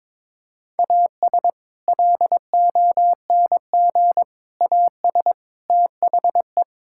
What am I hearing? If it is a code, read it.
Morse code, heard as a keyed tone, about 21 wpm, 700 Hz. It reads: AS LONG AS THE
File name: practice / 22wpm / Sentences with 4-Word Phrases - 22wpm